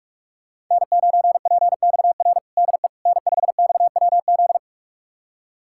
Morse code, heard as a keyed tone, about 47 wpm, 700 Hz. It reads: N9PXR DE N5XWZ
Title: USA Callsign DE Callsign - 5 Characters - 45wpm